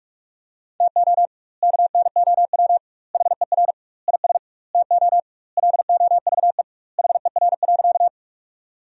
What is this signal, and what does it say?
Morse code, heard as a keyed tone, about 46 wpm, 700 Hz. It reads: TO KNOW HER IS TO LOVE HER.
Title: 2x Word-Spacing - Sentences from Top 400 Words - 45wpm